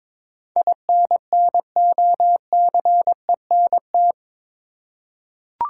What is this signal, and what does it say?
Morse code, heard as a keyed tone, about 22 wpm, 700 Hz. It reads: INNOCENT
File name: practice / 22wpm / Eight Letter Words - 22wpm